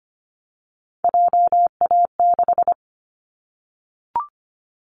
Morse code, heard as a keyed tone, about 25 wpm, 700 Hz. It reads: JA6
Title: Callsign Prefixes for Top 25 DXCC - 25wpm